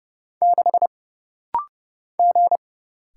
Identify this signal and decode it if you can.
Morse code, heard as a keyed tone, about 30 wpm, 700 Hz. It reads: 6 G
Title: Single Letter-Number - Rapid-Fire 30wpm